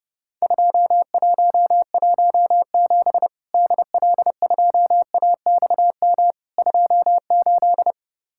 Morse code, heard as a keyed tone, about 29 wpm, 700 Hz. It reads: 2117 DL2AXM 28
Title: Worked All Europe DX Contest - QTC Exchange - 30wpm